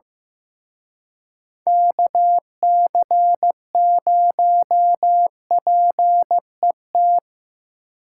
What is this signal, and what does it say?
Morse code, heard as a keyed tone, about 18 wpm, 700 Hz. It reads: KC0PET
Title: Speed Racing - USA 2x3 Call Signs 15wpm